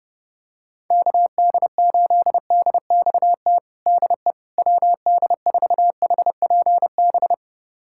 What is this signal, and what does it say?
Morse code, heard as a keyed tone, about 30 wpm, 700 Hz. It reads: KD8DXT DE WD4HPB
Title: USA Callsign DE Callsign - 6 Characters - 30wpm